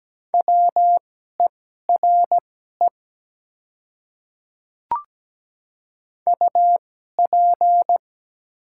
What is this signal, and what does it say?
Morse code, heard as a keyed tone, about 17 wpm, spaced as for 12 wpm, 700 Hz. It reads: WERE UP
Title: 2x Character Spacing - Top 100 Words - 17wpm